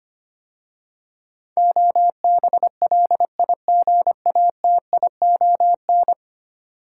Morse code, heard as a keyed tone, about 25 wpm, 700 Hz. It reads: OBLIGATION
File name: 10 Letter Words - First 3 Letters Given - 25wpm